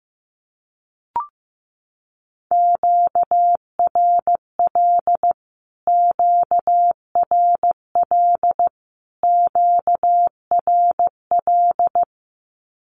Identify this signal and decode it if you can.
Morse code, heard as a keyed tone, about 15 wpm, 700 Hz. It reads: QRL QRL QRL
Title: Q-Codes Repeated 3x 15wpm